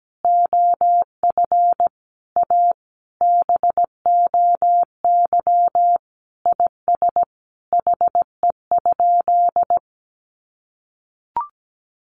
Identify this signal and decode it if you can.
Morse code, heard as a keyed tone, about 17 wpm, 700 Hz. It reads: OF A BOY IS HE?
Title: Sentences from Top 1300 Words 17wpm